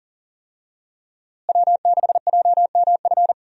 Morse code, heard as a keyed tone, about 40 wpm, 700 Hz. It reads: W6JMF
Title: USA Call Signs - 5 Characters - Repeated 3x - 40wpm